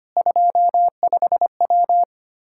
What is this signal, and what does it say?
Morse code, heard as a keyed tone, about 25 wpm, 700 Hz. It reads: 25W
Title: Practice QSO - Exchange 2 - Info - 25wpm